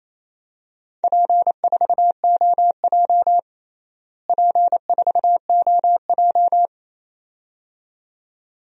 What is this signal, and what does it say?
Morse code, heard as a keyed tone, about 28 wpm, 700 Hz. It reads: P4OJ P4OJ
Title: DXCC-100 Call Signs - 4 Characters - Repeated 2x - 28wpm